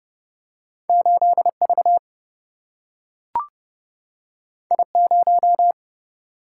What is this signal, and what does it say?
Morse code, heard as a keyed tone, about 30 wpm, 700 Hz. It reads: 8V I0
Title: Single-Letter Single-Number 30wpm